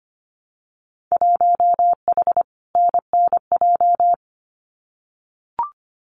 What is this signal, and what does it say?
Morse code, heard as a keyed tone, about 25 wpm, 700 Hz. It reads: 1H NNJ
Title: Winter Field Day Exchanges 25wpm